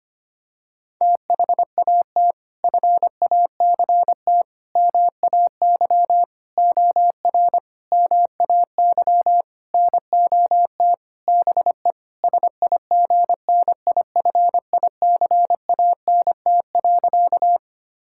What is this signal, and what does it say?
Morse code, heard as a keyed tone, about 25 wpm, 700 Hz. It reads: THAT FACT MAY OR MAY NOT BE SIGNIFICANT.
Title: Sentences from Top 1000 Words - Review Part 6 - 25wpm